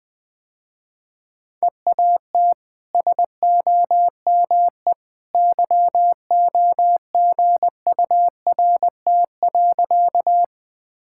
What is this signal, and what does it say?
Morse code, heard as a keyed tone, about 20 wpm, 700 Hz. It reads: EAT SOME YOGURT.
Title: Easy Sentences with Hints - 20wpm